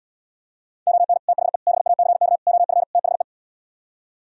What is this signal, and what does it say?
Morse code, heard as a keyed tone, about 77 wpm, 700 Hz. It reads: ON THE BACK OF THE